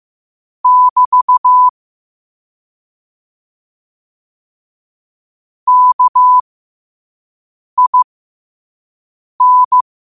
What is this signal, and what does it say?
Morse code, heard as a keyed tone, about 15 wpm, spaced as for 5 wpm, 1000 Hz. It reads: <BT> KIN
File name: qotdfiles / 5wpm